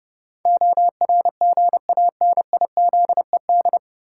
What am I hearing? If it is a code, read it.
Morse code, heard as a keyed tone, about 32 wpm, 700 Hz. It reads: ORGANIZED
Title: Sentences from Top 1300 Words - New Word Drill Alt - 30wpm